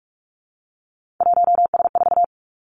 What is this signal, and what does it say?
Morse code, heard as a keyed tone, about 46 wpm, 700 Hz. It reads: 1S4
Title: Three Challenging Letter-Number Combinations 45wpm